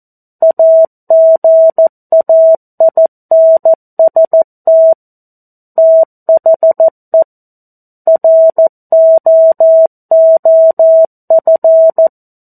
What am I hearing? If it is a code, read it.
Morse code, heard as a keyed tone, about 14 wpm, 650 Hz. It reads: AGAINST THE ROOF
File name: 02-Chapter-02 14 WPM